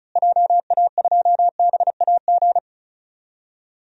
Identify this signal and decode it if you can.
Morse code, heard as a keyed tone, about 34 wpm, 700 Hz. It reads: JA2BAG
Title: DXCC-100 Call Signs - 6 Characters - Repeated 3x - 35wpm